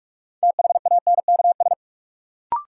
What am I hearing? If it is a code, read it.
Morse code, heard as a keyed tone, about 41 wpm, 700 Hz. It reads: THANKS
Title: Sentences from Top 600 Words - New Word Drill - 45wpm